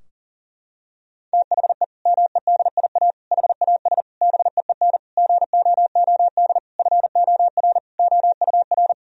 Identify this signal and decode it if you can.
Morse code, heard as a keyed tone, about 40 wpm, 700 Hz. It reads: THE MEDIA HAS BEEN GOOD FOR OUR
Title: Sentences from Top 1000 Words - Review Part 4 - 40wpm